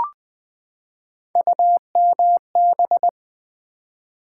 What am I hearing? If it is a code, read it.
Morse code, heard as a keyed tone, about 20 wpm, 690 Hz. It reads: UMB
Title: Build A Word - 3 Letters At A Time - 8-Letter Words - 20wpm